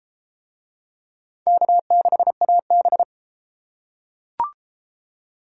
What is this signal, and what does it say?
Morse code, heard as a keyed tone, about 31 wpm, 700 Hz. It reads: K6AB